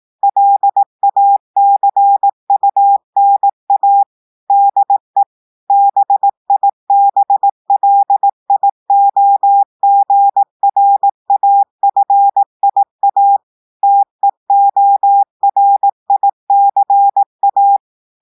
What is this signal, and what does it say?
Morse code, heard as a keyed tone, about 18 wpm, 800 Hz. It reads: LACUNA DE BIBLIOGRAFIA TEORICA
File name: audios / portugais / portugais texte en clair 1 vitesse 18 mots minutes